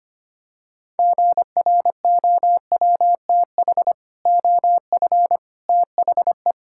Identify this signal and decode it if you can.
Morse code, heard as a keyed tone, about 25 wpm, 700 Hz. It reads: GROWTH OF THE
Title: Sentences from Top 1000 Words - Review Part 5 - 25wpm